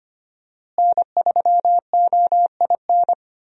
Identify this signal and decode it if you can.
Morse code, heard as a keyed tone, about 25 wpm, 700 Hz. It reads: N3OIN
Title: USA Call Signs - 5 Characters - Repeated 3x - 25wpm